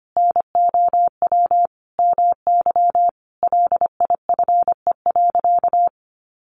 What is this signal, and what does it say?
Morse code, heard as a keyed tone, about 25 wpm, 700 Hz. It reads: NOW MY LIFE.
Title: Sentences from Top 1200 Words 25wpm